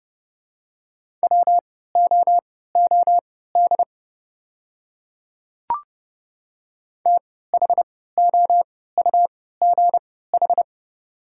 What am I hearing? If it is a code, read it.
Morse code, heard as a keyed tone, about 30 wpm, spaced as for 17 wpm, 700 Hz. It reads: WOOD THOUGH
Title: 3x Character Spacing - Top 500 Words - 30wpm